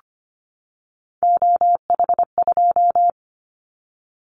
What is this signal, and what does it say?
Morse code, heard as a keyed tone, about 25 wpm, 700 Hz. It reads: OH2
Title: Callsign Prefixes for Top 25 DXCC - 25wpm